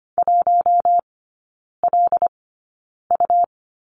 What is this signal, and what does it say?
Morse code, heard as a keyed tone, about 25 wpm, spaced as for 9 wpm, 700 Hz. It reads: 1LU
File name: Lesson 17e - Letter u - Three Characters 25wpm